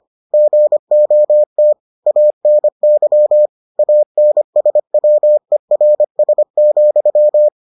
Morse code, heard as a keyed tone, about 25 wpm, 600 Hz. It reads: GOT ANY ANSWERS,